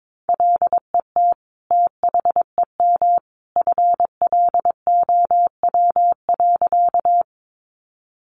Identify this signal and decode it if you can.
Morse code, heard as a keyed tone, about 22 wpm, 700 Hz. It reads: LET THEM FLOW.
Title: Sentences from Top 1200 Words 22wpm